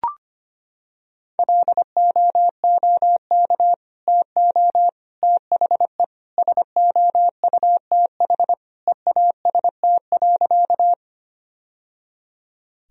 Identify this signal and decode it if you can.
Morse code, heard as a keyed tone, about 25 wpm, 700 Hz. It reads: LOOK TO THE SOUTH EAST.
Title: Sentences from Top 1000 Words - Review Part 3 - 25wpm